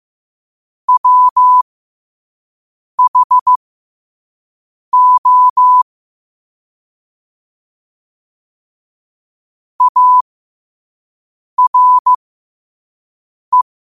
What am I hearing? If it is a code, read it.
Morse code, heard as a keyed tone, about 15 wpm, spaced as for 5 wpm, 1000 Hz. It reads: WHO ARE
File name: qotdfiles / 5wpm